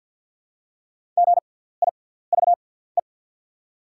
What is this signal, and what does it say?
Morse code, heard as a keyed tone, about 48 wpm, spaced as for 17 wpm, 690 Hz. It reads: GIVE
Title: 6x Character Spacing - Top 300 Words - 50wpm